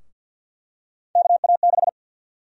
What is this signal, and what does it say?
Morse code, heard as a keyed tone, about 51 wpm, 700 Hz. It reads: KA6
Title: USA Call Sign Prefixes - Repeated 2x - 50wpm